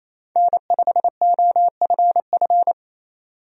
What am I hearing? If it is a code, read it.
Morse code, heard as a keyed tone, about 28 wpm, 700 Hz. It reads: N5OFF
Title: USA Call Signs - 5 Characters - 28wpm